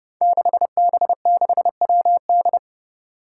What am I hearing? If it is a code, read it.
Morse code, heard as a keyed tone, about 31 wpm, 700 Hz. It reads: 6B6WD
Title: Five Challenging Letter-Number Combinations 30wpm